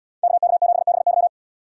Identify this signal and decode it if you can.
Morse code, heard as a keyed tone, about 77 wpm, 700 Hz. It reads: KK7BY